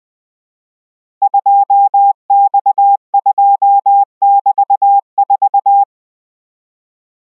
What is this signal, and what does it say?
Morse code, heard as a keyed tone, about 20 wpm, 800 Hz. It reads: 2X2<BT>4